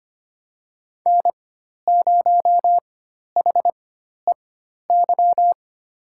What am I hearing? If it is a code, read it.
Morse code, heard as a keyed tone, about 25 wpm, spaced as for 12 wpm, 700 Hz. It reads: N0HEY
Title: USA Call Signs - 5 Characters - 4x Character Spacing - 25wpm